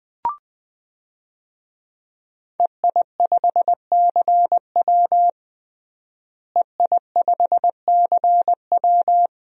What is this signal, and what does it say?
Morse code, heard as a keyed tone, about 21 wpm, 700 Hz. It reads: EI5CW EI5CW
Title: DXCC-100 Call Signs - 5 Characters - Repeated 2x - 20wpm